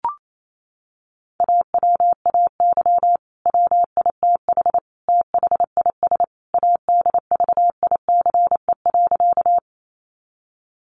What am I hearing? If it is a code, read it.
Morse code, heard as a keyed tone, about 28 wpm, 700 Hz. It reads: AWAY WITH THIS ADVICE.
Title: Sentences from Top 1100 Words 28wpm